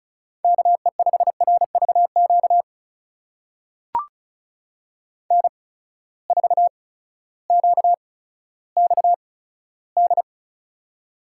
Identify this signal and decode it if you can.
Morse code, heard as a keyed tone, about 36 wpm, spaced as for 16 wpm, 700 Hz. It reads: KE5RVQ N4QXD
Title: USA Call Signs - 8x Character Spacing - 35wpm